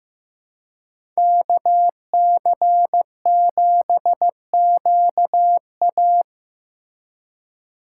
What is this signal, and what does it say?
Morse code, heard as a keyed tone, about 15 wpm, 700 Hz. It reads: KC7QA
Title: USA Call Signs - 5 Characters - 3x Character Spacing - 15wpm